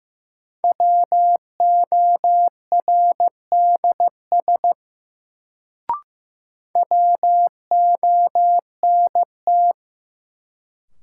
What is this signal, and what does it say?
Morse code, heard as a keyed tone, about 15 wpm, 700 Hz. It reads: WORDS WONT
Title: Sentences from Top 1000 Words - Review Part 6 - New Word Drill - 15wpm